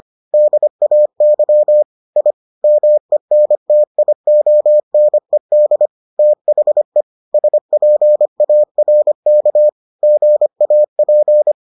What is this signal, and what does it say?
Morse code, heard as a keyed tone, about 25 wpm, 600 Hz. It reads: DAY I MENTIONED THE SPARK GAP